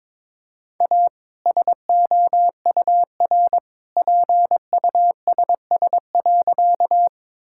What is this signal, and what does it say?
Morse code, heard as a keyed tone, about 22 wpm, 700 Hz. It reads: A SOUR PUSS.
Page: morse-code-ninja-practice-22wpm